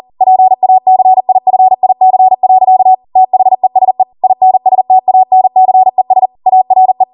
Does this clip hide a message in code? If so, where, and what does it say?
Yes: Morse code from 0.2 to 7.0 s, heard as a keyed tone, about 40 wpm, 750 Hz. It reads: PACIFIC. THESE INSTANCES ARE